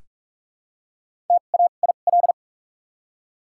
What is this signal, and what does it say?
Morse code, heard as a keyed tone, about 42 wpm, spaced as for 31 wpm, 700 Hz. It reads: TAIL
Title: 2x Character Spacing - Top 500 Words - 45wpm